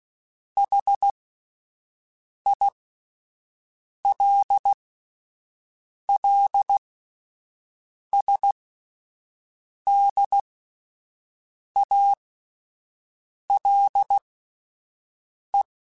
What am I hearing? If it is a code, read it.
Morse code, heard as a keyed tone, about 16 wpm, spaced as for 5 wpm, 800 Hz. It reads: HILLSDALE